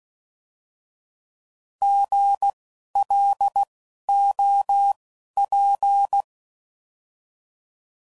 Morse code, heard as a keyed tone, about 16 wpm, spaced as for 11 wpm, 800 Hz. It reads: GLOP
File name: Prefixes / By First Letter